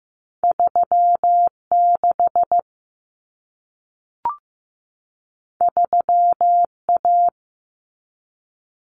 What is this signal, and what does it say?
Morse code, heard as a keyed tone, about 15 wpm, 700 Hz. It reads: 36 3A
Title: Common Cut Numbers - 2-Digit 15wpm